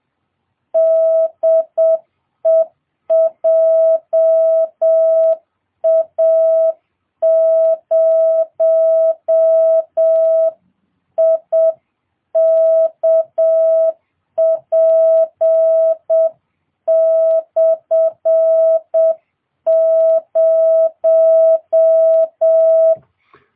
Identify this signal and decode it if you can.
Morse code, heard as a keyed tone, about 7 wpm, 660 Hz. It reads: DE JA0IKP/0